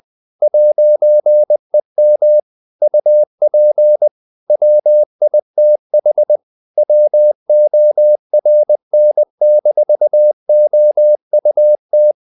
Morse code, heard as a keyed tone, about 20 wpm, 600 Hz. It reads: 'EM UP WITH WORN-OUT
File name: CW / Practice / If20